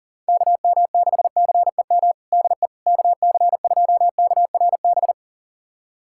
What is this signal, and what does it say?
Morse code, heard as a keyed tone, about 40 wpm, 690 Hz. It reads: KM6CEM DE KC2KRB
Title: USA Callsign DE Callsign - 6 Characters - 40wpm